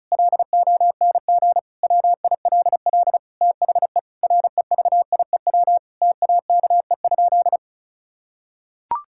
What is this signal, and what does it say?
Morse code, heard as a keyed tone, about 34 wpm, 700 Hz. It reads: LONG WILL THE REVIEW TAKE?
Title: Sentences from Top 1000 Words - Review Part 6 - 35wpm